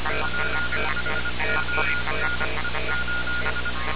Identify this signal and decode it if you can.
Morse code, heard as a keyed tone, about 25 wpm, 1430 Hz, with noise. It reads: PHONE:
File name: radiocap / REC z